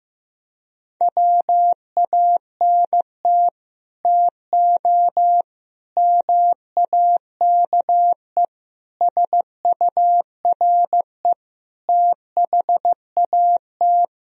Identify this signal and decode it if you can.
Morse code, heard as a keyed tone, about 15 wpm, 700 Hz. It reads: WANT TO MAKE SURE THAT
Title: Speed-Racing - Sets of 5 Words 15wpm